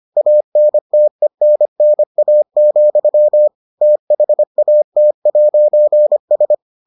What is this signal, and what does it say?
Morse code, heard as a keyed tone, about 25 wpm, 600 Hz. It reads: ANTENNA, THAT'S